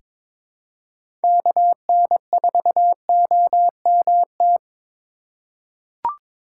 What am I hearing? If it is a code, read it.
Morse code, heard as a keyed tone, about 22 wpm, 700 Hz. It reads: KN4OMT
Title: USA Call Signs - 6 Characters - 4x Character Spacing - 22wpm